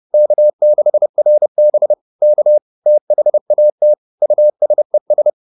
Morse code, heard as a keyed tone, about 30 wpm, 600 Hz. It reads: K6RB K THAT USES